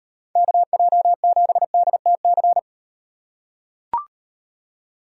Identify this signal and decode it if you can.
Morse code, heard as a keyed tone, about 34 wpm, 700 Hz. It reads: KJ7DTC